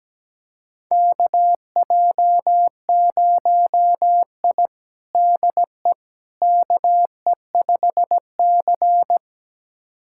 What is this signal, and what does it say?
Morse code, heard as a keyed tone, about 17 wpm, 700 Hz. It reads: KJ0I DE KE5C
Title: USA Callsign DE Callsign - 4 Characters - 17wpm